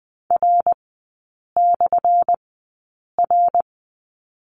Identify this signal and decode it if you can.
Morse code, heard as a keyed tone, about 20 wpm, spaced as for 8 wpm, 700 Hz. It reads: R/R
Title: 178 Lesson 29e - Slash - Three Characters 20wpm